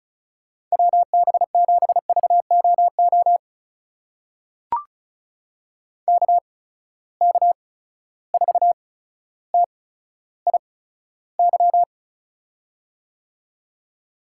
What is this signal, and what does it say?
Morse code, heard as a keyed tone, about 34 wpm, spaced as for 15 wpm, 700 Hz. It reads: WB7VOO KK4TIY
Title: USA Call Signs - 6 Characters - 8x Character Spacing - 35wpm